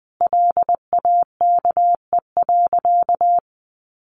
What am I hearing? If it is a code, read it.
Morse code, heard as a keyed tone, about 20 wpm, 700 Hz. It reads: LAKE.
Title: Sentences from Top 1300 Words 20wpm